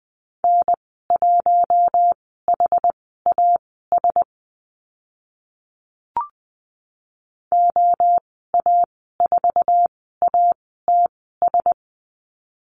Morse code, heard as a keyed tone, about 20 wpm, spaced as for 14 wpm, 700 Hz. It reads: N1HAS OA4ATS
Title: 069 Lesson 12g - Letter d - Call Signs 20wpm